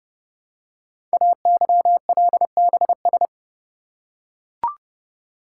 Morse code, heard as a keyed tone, about 29 wpm, 700 Hz. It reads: AYLBS
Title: Code Groups - 5 Letter:Numbers - 30wpm